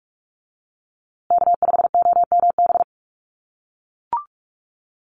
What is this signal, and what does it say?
Morse code, heard as a keyed tone, about 44 wpm, 700 Hz. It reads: K5OMB
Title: USA Call Signs Repeated 2x 45wpm